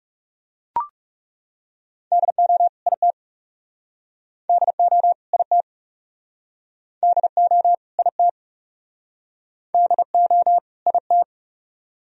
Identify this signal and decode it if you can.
Morse code, heard as a keyed tone, about 39 wpm, 700 Hz. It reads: DO IT DO IT DO IT DO IT